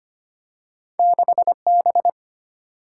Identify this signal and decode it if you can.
Morse code, heard as a keyed tone, about 25 wpm, 700 Hz. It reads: 6B
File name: Two Letter-Number Combinations - 25wpm